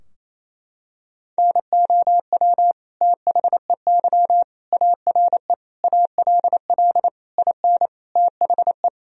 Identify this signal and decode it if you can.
Morse code, heard as a keyed tone, about 28 wpm, 700 Hz. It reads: NOW THEY ARE ALL IN THE
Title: Sentences from Top 1000 Words - Review Part 2 - 28wpm